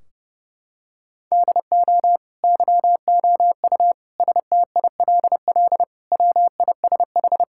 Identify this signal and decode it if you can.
Morse code, heard as a keyed tone, about 32 wpm, 700 Hz. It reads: DO YOU STILL WISH
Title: Sentences from Top 1600 Words 30wpm